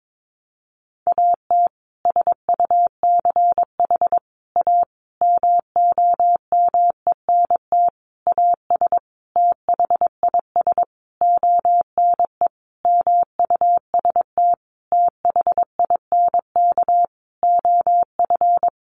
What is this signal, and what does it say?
Morse code, heard as a keyed tone, about 22 wpm, 700 Hz. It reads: AT SUCH A MOMENT AS THIS ONE MUST THINK OF
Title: Sentences from Top 1000 Words 22wpm